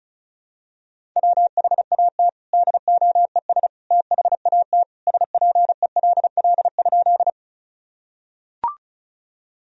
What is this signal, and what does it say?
Morse code, heard as a keyed tone, about 34 wpm, 700 Hz. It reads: WHAT DOES THAT SPELL?
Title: Sentences from Top 1000 Words - Review Part 1 - 35wpm